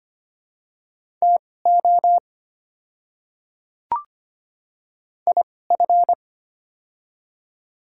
Morse code, heard as a keyed tone, about 25 wpm, spaced as for 18 wpm, 700 Hz. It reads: TO IF
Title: 2x Character Spacing - Top 100 Words - 25wpm